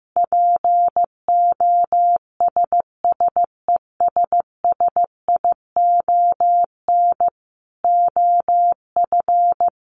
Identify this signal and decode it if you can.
Morse code, heard as a keyed tone, about 15 wpm, 700 Hz. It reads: POSSESSION OF
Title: Sentences from Top 1400 Words 15wpm